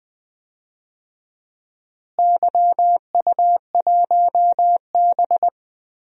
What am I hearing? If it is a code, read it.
Morse code, heard as a keyed tone, about 21 wpm, 700 Hz. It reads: YU1B
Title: DXCC-100 Call Signs - 4 Characters - Repeated 3x - 20wpm